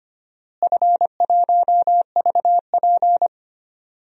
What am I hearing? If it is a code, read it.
Morse code, heard as a keyed tone, about 25 wpm, 700 Hz. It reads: F1VP
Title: DXCC-100 Call Signs - 4 Characters - Repeated 2x - 25wpm